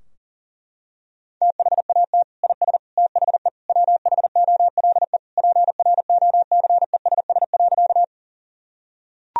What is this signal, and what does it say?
Morse code, heard as a keyed tone, about 40 wpm, 700 Hz. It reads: THAT IS THE WHOLE PROCESS.
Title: Sentences from Top 1000 Words - Review Part 3 - 40wpm